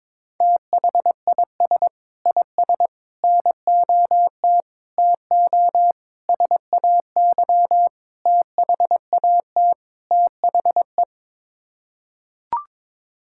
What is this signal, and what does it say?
Morse code, heard as a keyed tone, about 22 wpm, 700 Hz. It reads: THIS IS NOT TO SAY THAT THE
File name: practice / 22wpm / Sets of 7 Words - 22wpm